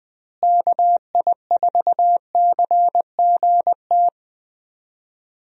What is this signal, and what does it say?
Morse code, heard as a keyed tone, about 20 wpm, 700 Hz. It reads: KI4CGT
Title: USA Call Signs - 6 Characters - Repeated 2x - 20wpm